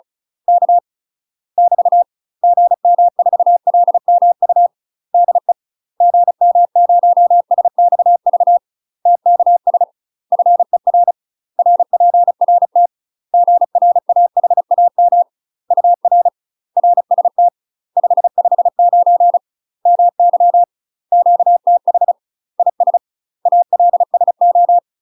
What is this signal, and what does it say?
Morse code, heard as a keyed tone, about 36 wpm, 700 Hz. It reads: K <BT> GM4LMU DE GM0SXV TKS FER RPRT GRAHAM UR RST 559 MY QTH IS ALSO
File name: UK MASTER QSOs TRAINING / 35WPM / UK MASTER QSOs 0011